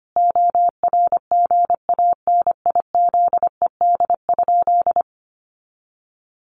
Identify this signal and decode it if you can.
Morse code, heard as a keyed tone, about 25 wpm, 700 Hz. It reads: ORGANIZED?
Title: Sentences from Top 1300 Words 25wpm